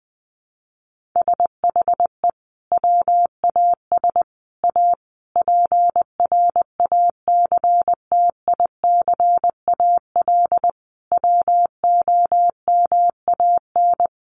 Morse code, heard as a keyed tone, about 20 wpm, 700 Hz. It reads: SHE WAS A PRACTICAL WOMAN
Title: Sentences from Top 1400 Words 20wpm